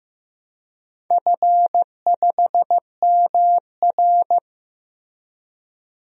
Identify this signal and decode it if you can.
Morse code, heard as a keyed tone, about 15 wpm, 700 Hz. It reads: F5MR
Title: DXCC-100 Call Signs - 4 Characters - Repeated 3x - 15wpm